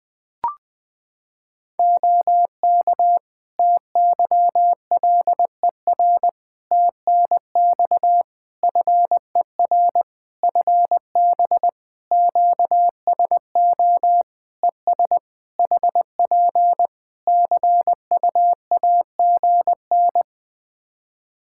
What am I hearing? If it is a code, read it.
Morse code, heard as a keyed tone, about 20 wpm, 700 Hz. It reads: OK TYLER TNX FER FB QSO ES HP CUAGN
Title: Practice QSO - Exchange 4 - Ending - 20wpm